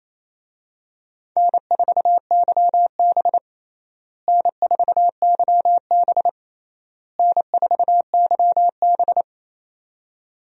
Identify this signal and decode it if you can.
Morse code, heard as a keyed tone, about 28 wpm, 700 Hz. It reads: N4YB N4YB N4YB
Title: USA Call Signs - 4 Characters - Repeated 3x - 28wpm